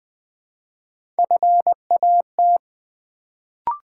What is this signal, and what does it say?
Morse code, heard as a keyed tone, about 20 wpm, 700 Hz. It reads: FAT
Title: Build A Word - 1 Letter At A Time - 3-Letter Words - 20wpm